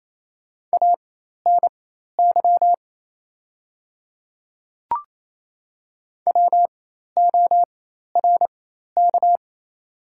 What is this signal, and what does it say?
Morse code, heard as a keyed tone, about 28 wpm, spaced as for 13 wpm, 700 Hz. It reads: ANY WORK
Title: 4x Character Spacing - Top 300 Words - 28wpm